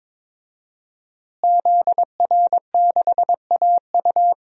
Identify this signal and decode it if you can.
Morse code, heard as a keyed tone, about 22 wpm, 700 Hz. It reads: ZR6AU
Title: DXCC-100 Call Signs - 5 Characters - Repeated 3x - 22wpm